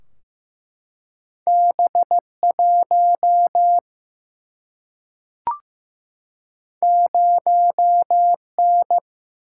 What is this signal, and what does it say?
Morse code, heard as a keyed tone, about 15 wpm, 700 Hz. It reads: B 1 0 N
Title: Single-Letter Single-Number 15wpm